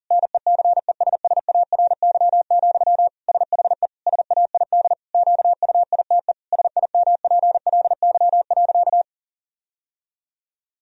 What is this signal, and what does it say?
Morse code, heard as a keyed tone, about 40 wpm, 700 Hz. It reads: NECESSARY, SHE SAID QUITE SIMPLY.
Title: Sentences from Top 1000 Words - Review Part 4 - 40wpm